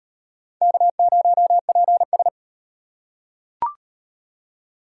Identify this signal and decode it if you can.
Morse code, heard as a keyed tone, about 37 wpm, 700 Hz. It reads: K0PS